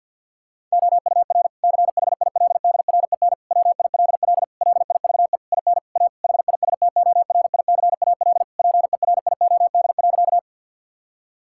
Morse code, heard as a keyed tone, about 52 wpm, 700 Hz. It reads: OUR CHILDREN WILL LIVE IN A HISTORICAL PERIOD.
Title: Sentences from Top 1000 Words - Review Part 5 - 50wpm